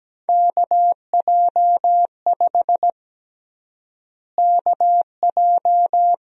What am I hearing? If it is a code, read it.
Morse code, heard as a keyed tone, about 17 wpm, 700 Hz. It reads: KJ5 KJ
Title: USA Call Sign Prefixes - Repeated 2x - 17wpm